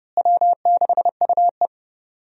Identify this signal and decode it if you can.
Morse code, heard as a keyed tone, about 32 wpm, 700 Hz. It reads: W6UE
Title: USA Call Signs - 4 Characters - Repeated 3x - 30wpm